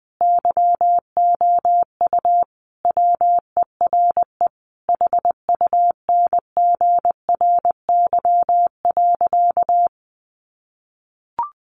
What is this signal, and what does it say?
Morse code, heard as a keyed tone, about 19 wpm, 700 Hz. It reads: YOU WERE HUNGRY.
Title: Sentences from Top 1400 Words 20wpm